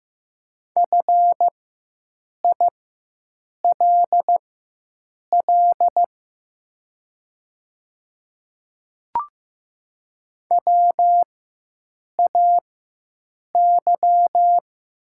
Morse code, heard as a keyed tone, about 15 wpm, spaced as for 7 wpm, 700 Hz. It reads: FILL WAY
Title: 4x Character Spacing - Top 500 Words - 15wpm